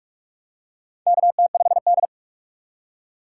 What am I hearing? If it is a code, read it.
Morse code, heard as a keyed tone, about 44 wpm, 690 Hz. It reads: KT5D
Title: USA Call Signs - 6x Character Spacing - 45wpm